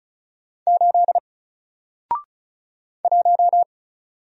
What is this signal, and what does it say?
Morse code, heard as a keyed tone, about 34 wpm, 700 Hz. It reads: 8 1
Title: Single Letter-Number Alt 35wpm